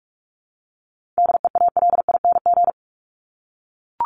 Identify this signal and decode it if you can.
Morse code, heard as a keyed tone, about 44 wpm, 700 Hz. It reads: DEALING
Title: ING Suffix - Top 100 Words with Base Words 45wpm